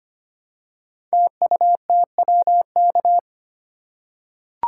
Morse code, heard as a keyed tone, about 25 wpm, 700 Hz. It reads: TUTWK
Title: Code Groups - 5 Letters - 25wpm